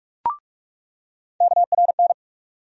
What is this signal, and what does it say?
Morse code, heard as a keyed tone, about 43 wpm, 700 Hz. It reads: KRN
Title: USA Call Sign Postfixes - 3 Letters 45wpm